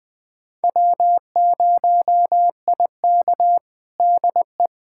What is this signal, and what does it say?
Morse code, heard as a keyed tone, about 20 wpm, 700 Hz. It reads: W0IK DE
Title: USA Callsign DE Callsign - 4 Characters - 20wpm